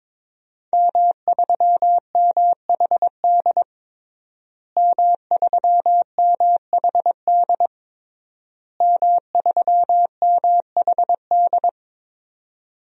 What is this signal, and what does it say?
Morse code, heard as a keyed tone, about 21 wpm, 700 Hz. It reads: M3MHD M3MHD M3MHD
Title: DXCC-100 Call Signs - 5 Characters - Repeated 3x - 22wpm